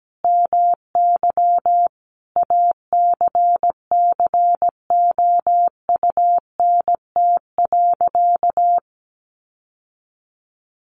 Morse code, heard as a keyed tone, about 17 wpm, 700 Hz. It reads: MY ACCOUNT.
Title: Sentences from Top 900 Words 17wpm